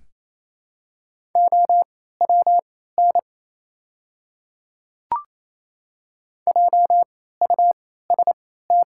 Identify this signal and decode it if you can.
Morse code, heard as a keyed tone, about 27 wpm, spaced as for 16 wpm, 700 Hz. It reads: OWN JUST
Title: 3x Character Spacing - Top 300 Words - 28wpm